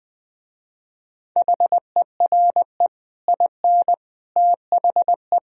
Morse code, heard as a keyed tone, about 20 wpm, 700 Hz. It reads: HERE IN THE
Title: Speed-Racing - Sets of 5 Words 17wpm